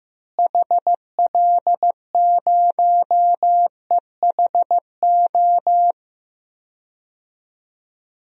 Morse code, heard as a keyed tone, about 15 wpm, 700 Hz. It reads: HL0EHO
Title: DXCC-100 Call Signs - 6 Characters - Repeated 3x - 15wpm